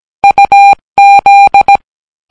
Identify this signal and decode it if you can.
Morse code, heard as a keyed tone, about 17 wpm, 800 Hz. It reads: UZ